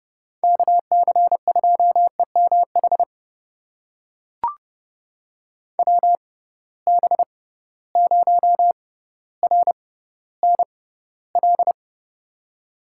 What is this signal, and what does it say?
Morse code, heard as a keyed tone, about 30 wpm, spaced as for 15 wpm, 700 Hz. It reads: KC2EMH WB0RNL
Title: USA Call Signs - 6 Characters - 6x Character Spacing - 30wpm